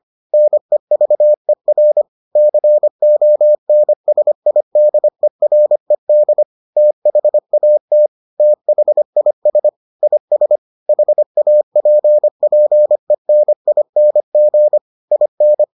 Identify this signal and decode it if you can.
Morse code, heard as a keyed tone, about 25 wpm, 600 Hz. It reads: NEVER CONSIDERED THAT THIS IS HAPPENING IN